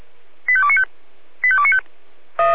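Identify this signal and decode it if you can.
Five-tone selective call (ZVEI), heard as quick runs of five tones, tones of about 70 ms. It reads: 85295 85295
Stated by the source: ZVEI five-tone selective call received by SDR.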